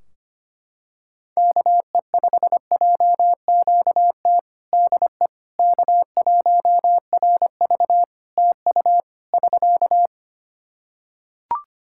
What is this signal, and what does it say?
Morse code, heard as a keyed tone, about 25 wpm, 700 Hz. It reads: KE5JQT DE K1RV TU <SK>
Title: Practice QSO - Exchange 4 - Ending - 25wpm